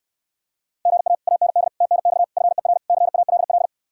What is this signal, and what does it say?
Morse code, heard as a keyed tone, about 74 wpm, 700 Hz. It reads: GA AND TNX FER CALL